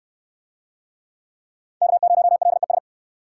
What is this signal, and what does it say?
Morse code, heard as a keyed tone, about 69 wpm, 700 Hz. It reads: K0LES